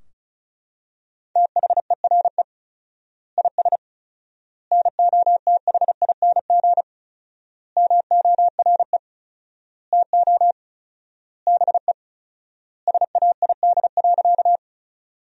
Morse code, heard as a keyed tone, about 36 wpm, 700 Hz. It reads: THERE IS NOTHING MORE TO BE SAID.
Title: 4x Word-Spacing - Sentences from Top 500 Words - 35wpm